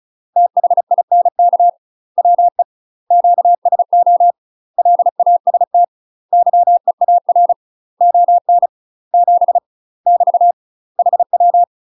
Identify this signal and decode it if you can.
Morse code, heard as a keyed tone, about 35 wpm, 700 Hz. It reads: THINK WE QSO LAST YEAR ON 7 <BT> HW
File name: UK MASTER QSOs TRAINING / 35WPM / UK MASTER QSOs 0038